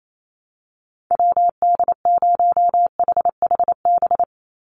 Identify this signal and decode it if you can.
Morse code, heard as a keyed tone, about 28 wpm, 700 Hz. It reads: WD0HHB
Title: USA Call Signs 28wpm